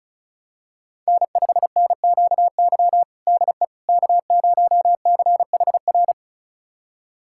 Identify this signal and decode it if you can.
Morse code, heard as a keyed tone, about 34 wpm, 700 Hz. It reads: N5NQY DE K0CHR
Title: USA Callsign DE Callsign - 5 Characters - 35wpm